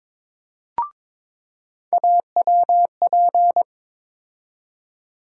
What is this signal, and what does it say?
Morse code, heard as a keyed tone, about 22 wpm, 700 Hz. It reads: AWP
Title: USA Call Sign Postfixes - 3 Letters 22wpm